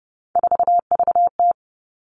Morse code, heard as a keyed tone, about 30 wpm, 700 Hz. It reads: 4VT
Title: All Cut Numbers - 3-Digit 30wpm